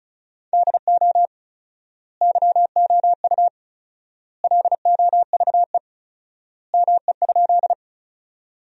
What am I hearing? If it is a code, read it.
Morse code, heard as a keyed tone, about 36 wpm, 700 Hz. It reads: DO YOU LOVE ME?
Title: 4x Word-Spacing - Sentences from Top 400 Words - 35wpm